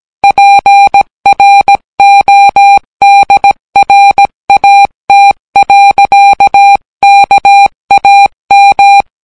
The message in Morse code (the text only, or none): PRODRAT. KAM